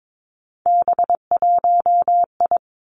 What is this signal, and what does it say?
Morse code, heard as a keyed tone, about 22 wpm, 700 Hz. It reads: B1I
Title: Three Challenging Letter-Number Combinations 22wpm